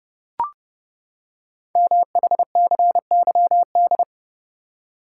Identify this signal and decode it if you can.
Morse code, heard as a keyed tone, about 28 wpm, 700 Hz. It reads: MHCYD
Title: Code Groups - 5 Letters - 30wpm